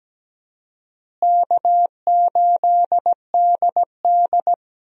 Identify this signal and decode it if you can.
Morse code, heard as a keyed tone, about 17 wpm, 700 Hz. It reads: K8DD
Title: USA Call Signs - 4 Characters - Repeated 3x - 17wpm